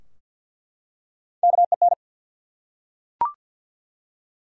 Morse code, heard as a keyed tone, about 48 wpm, 700 Hz. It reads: KEN
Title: USA Common Names - Max 3 Letters - 50wpm